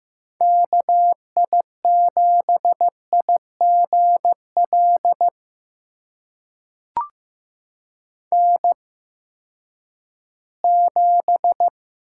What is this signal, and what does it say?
Morse code, heard as a keyed tone, about 15 wpm, spaced as for 10 wpm, 700 Hz. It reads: KI7IGL N7
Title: USA Call Signs - 8x Character Spacing - 15wpm